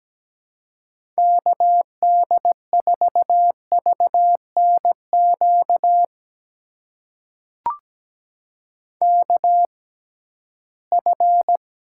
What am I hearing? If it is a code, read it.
Morse code, heard as a keyed tone, about 17 wpm, spaced as for 13 wpm, 700 Hz. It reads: KD4VNQ KF
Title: USA Call Signs - 6 Characters - 6x Character Spacing - 17wpm